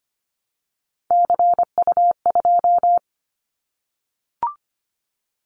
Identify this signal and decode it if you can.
Morse code, heard as a keyed tone, about 25 wpm, 700 Hz. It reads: CU2
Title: Callsign Prefixes for Top 50 DXCC - 25wpm